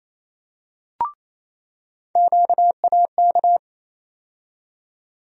Morse code, heard as a keyed tone, about 28 wpm, 700 Hz. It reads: QAK
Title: USA Call Sign Postfixes - 3 Letters 28wpm